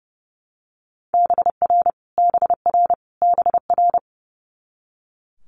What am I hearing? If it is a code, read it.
Morse code, heard as a keyed tone, about 30 wpm, 700 Hz. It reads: BR BR BR
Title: Multi-letter Phonemes Repeated 3x 30wpm